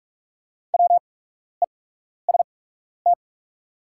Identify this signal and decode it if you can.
Morse code, heard as a keyed tone, about 45 wpm, spaced as for 12 wpm, 690 Hz. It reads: WEST
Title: 8x Character Spacing - Top 500 Words - 45wpm